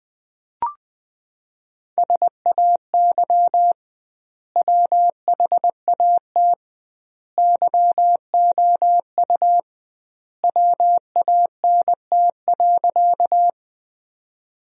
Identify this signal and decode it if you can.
Morse code, heard as a keyed tone, about 20 wpm, 700 Hz. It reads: SAY WHAT YOU WANT.
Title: 2x Word-Spacing - Sentences from Top 200 Words - 20wpm